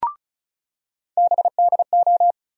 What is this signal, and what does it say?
Morse code, heard as a keyed tone, about 36 wpm, 700 Hz. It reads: BDO
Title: USA Call Sign Postfixes - 3 Letters 35wpm